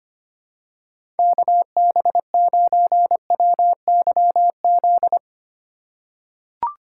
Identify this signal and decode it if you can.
Morse code, heard as a keyed tone, about 25 wpm, 700 Hz. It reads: KB9WYZ
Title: USA Call Signs - 4x Character Spacing - 25wpm